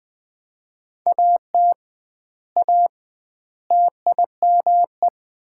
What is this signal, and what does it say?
Morse code, heard as a keyed tone, about 19 wpm, 700 Hz. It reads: AT A TIME
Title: 2x Word-Spacing - Sets of 4 Words - 20wpm